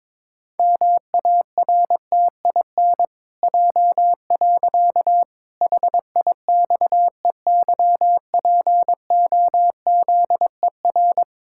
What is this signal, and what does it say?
Morse code, heard as a keyed tone, about 22 wpm, 700 Hz. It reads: MARTIN J. HIXEYPOZER
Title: Pangram Challenge v2 - 22wpm